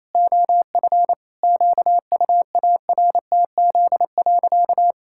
Morse code, heard as a keyed tone, about 28 wpm, 700 Hz. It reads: OF QUARTZ.
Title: Pangram Challenge v2 - 28wpm